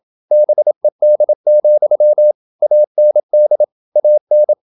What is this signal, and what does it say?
Morse code, heard as a keyed tone, about 27 wpm, 600 Hz. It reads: BED, AND AN
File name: Never 0027wpm